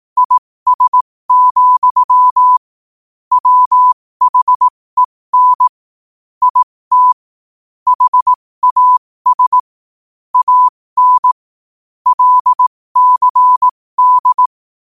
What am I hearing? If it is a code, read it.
Morse code, heard as a keyed tone, about 18 wpm, spaced as for 15 wpm, 1000 Hz. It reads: IS, WHEN IT HAS AN LCD
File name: qotdfiles / 15wpm